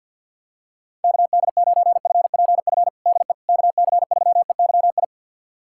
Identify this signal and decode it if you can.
Morse code, heard as a keyed tone, about 52 wpm, 700 Hz. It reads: KD9FPF DE KC3EYI
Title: USA Callsign DE Callsign - 6 Characters - 50wpm